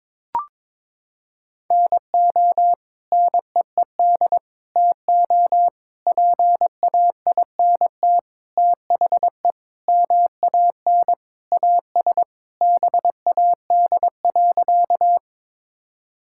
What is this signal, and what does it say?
Morse code, heard as a keyed tone, about 22 wpm, 700 Hz. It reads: NO NEED TO PAINT THE MAN AS BAD.
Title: Sentences from Top 1000 Words - Review Part 6 - 22wpm